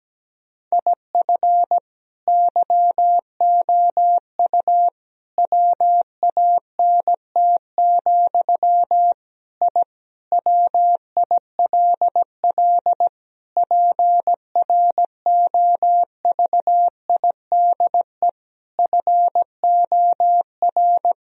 Morse code, heard as a keyed tone, about 17 wpm, 700 Hz. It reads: IF YOU WANT, I WILL PROVIDE FOR
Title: Sentences from Top 1000 Words - Review Part 3 - 17wpm